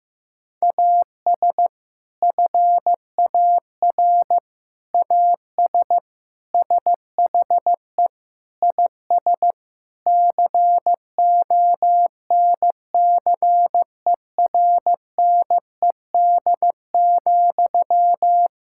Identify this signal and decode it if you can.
Morse code, heard as a keyed tone, about 16 wpm, 700 Hz. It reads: AS FAR AS SHE IS CONCERNED,
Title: Sentences from Top 1000 Words - Review Part 6 - 15wpm